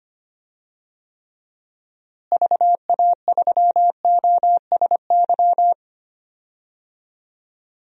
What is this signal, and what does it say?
Morse code, heard as a keyed tone, about 25 wpm, 700 Hz. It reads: VA3OSY
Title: DXCC-100 Call Signs - 6 Characters - Repeated 2x - 25wpm